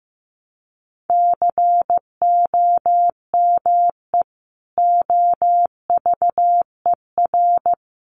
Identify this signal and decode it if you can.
Morse code, heard as a keyed tone, about 14 wpm, 700 Hz. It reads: COME OVER
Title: Sentences from Top 300 Words 15wpm